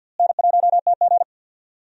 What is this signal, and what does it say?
Morse code, heard as a keyed tone, about 52 wpm, 700 Hz. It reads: N1TG
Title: USA Call Signs - 4 Characters - Repeated 3x - 50wpm